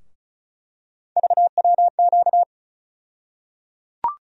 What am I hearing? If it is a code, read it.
Morse code, heard as a keyed tone, about 36 wpm, 700 Hz. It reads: VWQ
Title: USA Call Sign Postfixes - 3 Letters 35wpm